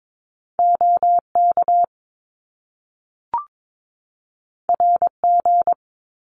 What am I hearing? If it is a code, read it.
Morse code, heard as a keyed tone, about 22 wpm, 700 Hz. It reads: OK RG
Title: Top Two Letter Combinations in Words 22wpm